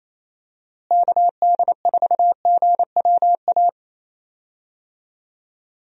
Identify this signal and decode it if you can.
Morse code, heard as a keyed tone, about 27 wpm, 700 Hz. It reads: KD4GWA
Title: USA Call Signs - 6 Characters - Repeated 2x - 28wpm